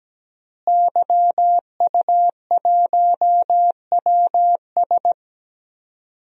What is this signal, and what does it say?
Morse code, heard as a keyed tone, about 17 wpm, 700 Hz. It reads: YU1WS
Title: DXCC-100 Call Signs - 5 Characters - Repeated 2x - 17wpm